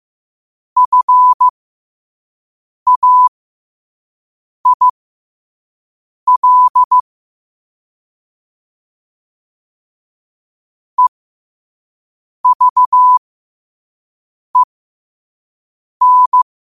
Morse code, heard as a keyed tone, about 15 wpm, spaced as for 5 wpm, 1000 Hz. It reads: FAIL EVEN
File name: qotdfiles / 5wpm